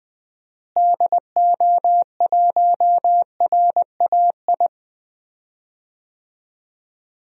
Morse code, heard as a keyed tone, about 21 wpm, 700 Hz. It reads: DO1RAI
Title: DXCC-100 Call Signs - 6 Characters - Repeated 2x - 20wpm